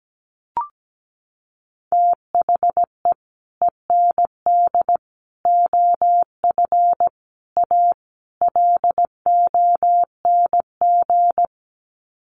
Morse code, heard as a keyed tone, about 17 wpm, 700 Hz. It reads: THE END OF A LONG
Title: Sets of 5 Words Encore2 17wpm